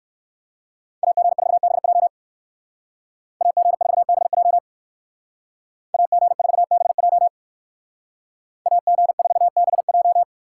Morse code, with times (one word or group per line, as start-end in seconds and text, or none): 1.0-2.1 AG4BJ
3.4-4.6 AG4BJ
5.9-7.3 AG4BJ
8.7-10.2 AG4BJ